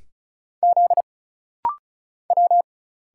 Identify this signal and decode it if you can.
Morse code, heard as a keyed tone, about 36 wpm, 700 Hz. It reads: Z W
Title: Single Letters - Rapid-Fire 35wpm